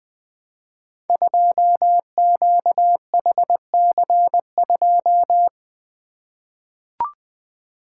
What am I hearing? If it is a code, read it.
Morse code, heard as a keyed tone, about 20 wpm, 700 Hz. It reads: 2QHC2
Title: Code Groups - 5 Letter:Numbers - 20wpm